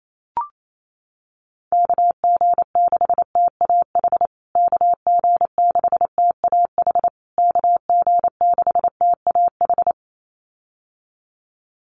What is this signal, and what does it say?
Morse code, heard as a keyed tone, about 28 wpm, 700 Hz. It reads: KG6TAH KG6TAH KG6TAH
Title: USA Call Signs Repeated 3x 28wpm